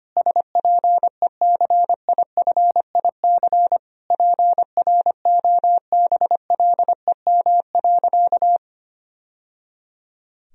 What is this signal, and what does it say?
Morse code, heard as a keyed tone, about 25 wpm, 700 Hz. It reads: SPECIFIC PROBLEM.
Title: Sentences from Top 1000 Words - Review Part 3 - 25wpm